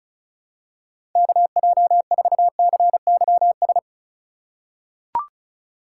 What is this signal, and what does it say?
Morse code, heard as a keyed tone, about 36 wpm, 700 Hz. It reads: KJ4CYS
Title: USA Call Signs - 3x Character Spacing - 35wpm